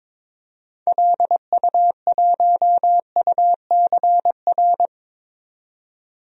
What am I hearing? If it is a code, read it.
Morse code, heard as a keyed tone, about 21 wpm, 700 Hz. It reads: LU1UCR
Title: DXCC-100 Call Signs - 6 Characters - 22wpm